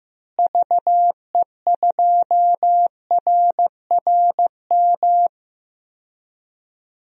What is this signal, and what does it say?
Morse code, heard as a keyed tone, about 15 wpm, 700 Hz. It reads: VE2RRM
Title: DXCC-100 Call Signs - 6 Characters - Repeated 2x - 15wpm